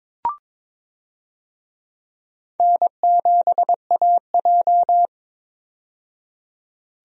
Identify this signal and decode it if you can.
Morse code, heard as a keyed tone, about 22 wpm, 700 Hz. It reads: N7AJ
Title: USA Call Signs - 4 Characters - 22wpm